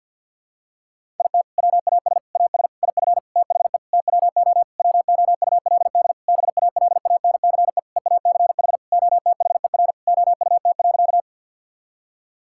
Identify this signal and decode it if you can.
Morse code, heard as a keyed tone, about 50 wpm, 700 Hz. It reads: IT WAS AS IF THE TWO WOULD BALANCE EACH OTHER OUT.
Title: Sentences from Top 1000 Words - Review Part 6 - 50wpm